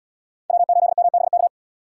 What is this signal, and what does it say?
Morse code, heard as a keyed tone, about 77 wpm, 700 Hz. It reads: W8MBK